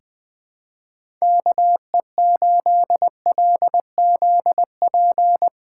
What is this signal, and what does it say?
Morse code, heard as a keyed tone, about 20 wpm, 700 Hz. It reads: KE8LZP
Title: USA Call Signs - 6 Characters - 8x Character Spacing - 20wpm